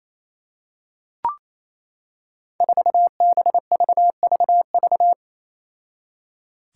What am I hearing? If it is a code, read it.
Morse code, heard as a keyed tone, about 28 wpm, 700 Hz. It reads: 4BVVV
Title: Five Challenging Letter-Number Combinations 28wpm